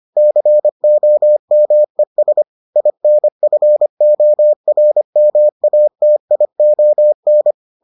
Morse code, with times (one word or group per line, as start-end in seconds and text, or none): 0.2-2.4 COMES
2.8-7.5 INFORMATION